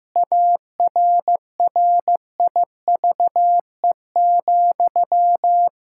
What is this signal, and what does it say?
Morse code, heard as a keyed tone, about 15 wpm, 700 Hz. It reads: ARRIVE,
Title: Pangram Challenge v2 - 15wpm